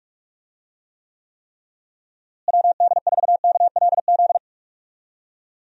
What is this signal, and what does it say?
Morse code, heard as a keyed tone, about 46 wpm, 700 Hz. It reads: WD4KLZ
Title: USA Call Signs - 6 Characters - Repeated 3x - 45wpm